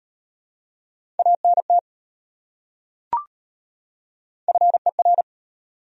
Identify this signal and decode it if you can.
Morse code, heard as a keyed tone, about 34 wpm, 700 Hz. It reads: ANT FER